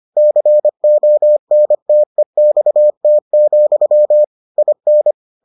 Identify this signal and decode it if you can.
Morse code, heard as a keyed tone, about 25 wpm, 600 Hz. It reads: CONTEXT, IN